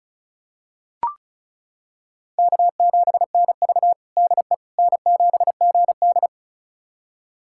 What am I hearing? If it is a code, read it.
Morse code, heard as a keyed tone, about 36 wpm, 700 Hz. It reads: K7NV DE N7GD
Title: USA Callsign DE Callsign - 4 Characters - 35wpm